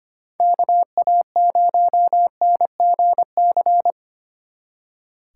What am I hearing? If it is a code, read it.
Morse code, heard as a keyed tone, about 30 wpm, 700 Hz. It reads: KA0NGC
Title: Speed Racing - USA Call Signs 25wpm